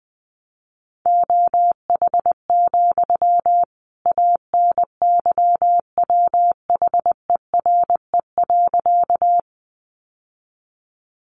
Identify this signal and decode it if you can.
Morse code, heard as a keyed tone, about 20 wpm, 700 Hz. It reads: OH, ANYWHERE.
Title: Sentences from Top 1400 Words 20wpm